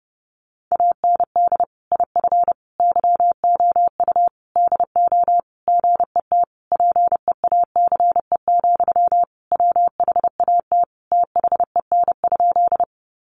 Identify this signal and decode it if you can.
Morse code, heard as a keyed tone, about 30 wpm, 700 Hz. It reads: AND IF YOU DO GET PEACE, WHAT THEN?
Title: Sentences from Top 1500 Words 30wpm